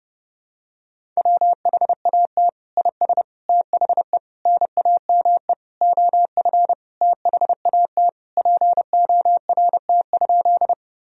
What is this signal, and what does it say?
Morse code, heard as a keyed tone, about 32 wpm, 700 Hz. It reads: WHAT IS THE NAME OF THAT PORT?
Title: Sentences from Top 1000 Words - Review Part 1 - 30wpm